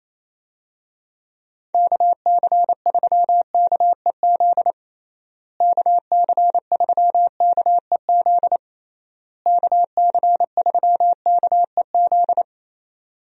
Morse code, heard as a keyed tone, about 28 wpm, 700 Hz. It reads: KC3KEZ KC3KEZ KC3KEZ
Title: USA Call Signs - 6 Characters - Repeated 3x - 28wpm